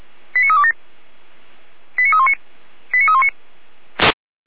Five-tone selective call (ZVEI), heard as quick runs of five tones, tones of about 70 ms. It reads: 89327 89319 89319